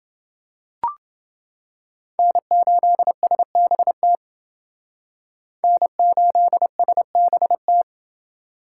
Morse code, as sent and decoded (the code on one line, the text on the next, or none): -. ---.. ... -... - / -. ---.. ... -... -
N8SBT N8SBT